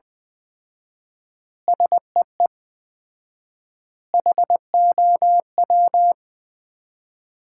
Morse code, heard as a keyed tone, about 20 wpm, 700 Hz. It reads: SEE HOW
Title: 4x Word-Spacing - Sets of 5 Words - 20wpm